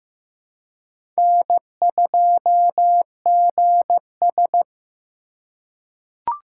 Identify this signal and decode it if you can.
Morse code, heard as a keyed tone, about 15 wpm, 700 Hz. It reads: N2GS
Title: USA Call Signs - 4 Characters - 3x Character Spacing - 15wpm